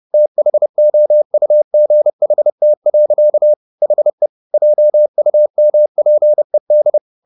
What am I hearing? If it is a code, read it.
Morse code, heard as a keyed tone, about 30 wpm, 600 Hz. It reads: THOUGHT. HE JUMPED
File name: the old man and the sea / the old man and the sea 1 30wpm